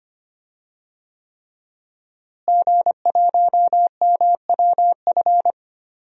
Morse code, heard as a keyed tone, about 25 wpm, 700 Hz. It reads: G1MWF
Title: DXCC-100 Call Signs - 5 Characters - Repeated 3x - 25wpm